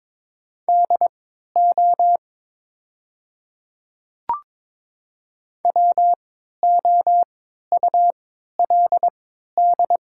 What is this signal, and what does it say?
Morse code, heard as a keyed tone, about 23 wpm, spaced as for 13 wpm, 700 Hz. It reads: DO WOULD
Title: 3x Character Spacing - Top 100 Words - 22wpm